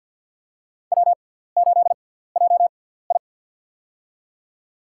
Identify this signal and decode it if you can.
Morse code, heard as a keyed tone, about 47 wpm, spaced as for 17 wpm, 700 Hz. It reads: W8JI
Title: USA Call Signs - 4 Characters - 6x Character Spacing - 50wpm